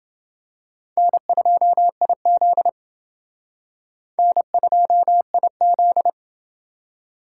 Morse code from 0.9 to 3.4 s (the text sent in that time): N2IZ